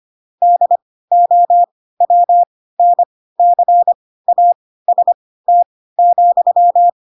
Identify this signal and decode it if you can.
Morse code, heard as a keyed tone, about 25 wpm, spaced as for 16 wpm, 700 Hz. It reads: DOWNCAST,
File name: Pinocchio / Pinocchio ch3 10 25 ik1hge